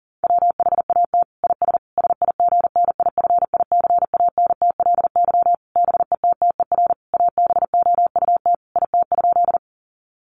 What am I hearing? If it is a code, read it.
Morse code, heard as a keyed tone, about 40 wpm, 700 Hz. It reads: WHAT IS SIGNIFICANTLY BETTER ABOUT IT?
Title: Sentences from Top 1500 Words 40wpm